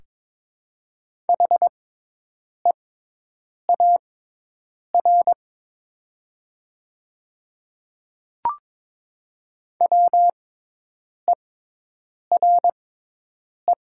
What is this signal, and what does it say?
Morse code, heard as a keyed tone, about 22 wpm, spaced as for 8 wpm, 690 Hz. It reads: HEAR WERE
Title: 6x Character Spacing - Top 300 Words - 22wpm